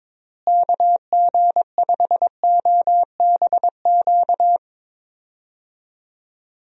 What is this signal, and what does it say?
Morse code, heard as a keyed tone, about 22 wpm, 700 Hz. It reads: KG5OBQ
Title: USA Call Signs - 6 Characters - Repeated 2x - 22wpm